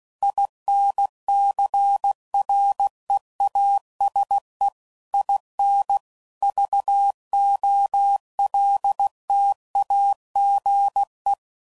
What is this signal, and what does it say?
Morse code, heard as a keyed tone, about 16 wpm, 800 Hz. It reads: INCREASE IN VOLTAGE
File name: CW Code Groups @15WPM / Technical Subjects / XC